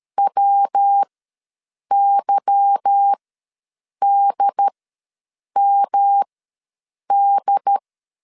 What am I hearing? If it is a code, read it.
Morse code, heard as a keyed tone, about 13 wpm, spaced as for 7 wpm, 800 Hz. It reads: WYDMD
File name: LEZIONI / 8 YJ45 / 8 YJ4 ws 12wpm 3249